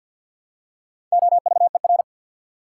Morse code, heard as a keyed tone, about 50 wpm, 700 Hz. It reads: OVER
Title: Prefixes and Suffixes 50wpm